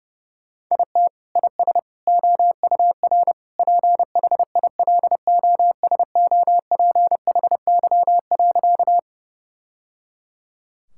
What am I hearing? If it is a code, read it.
Morse code, heard as a keyed tone, about 30 wpm, 700 Hz. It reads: IT IS OUR PHILOSOPHY.
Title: Sentences from Top 1000 Words - Review Part 6 - 30wpm